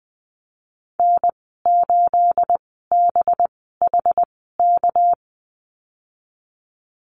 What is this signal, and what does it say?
Morse code, heard as a keyed tone, about 20 wpm, spaced as for 15 wpm, 700 Hz. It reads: N8BHK
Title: 213 Lesson 34g - Number 8 - Call Signs 20wpm